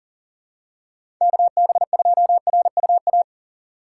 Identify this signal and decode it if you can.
Morse code, heard as a keyed tone, about 40 wpm, 700 Hz. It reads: KB2RUA
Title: USA Call Signs - 6 Characters - 3x Character Spacing - 40wpm